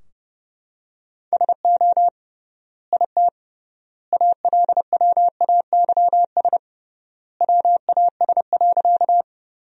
Morse code, heard as a keyed tone, about 30 wpm, 700 Hz. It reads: SO IT ALWAYS WAS.
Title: 3x Word-Spacing - Sentences from Top 300 Words - 30wpm